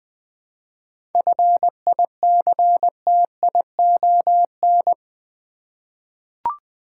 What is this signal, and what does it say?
Morse code, heard as a keyed tone, about 20 wpm, 700 Hz. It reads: FICTION
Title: 7 Letter Words - First 3 Letters Given - 20wpm